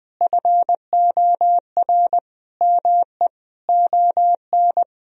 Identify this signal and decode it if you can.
Morse code, heard as a keyed tone, about 21 wpm, 700 Hz. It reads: FOR ME ON
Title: Sentences from Top 1000 Words - Review Part 2 - 20wpm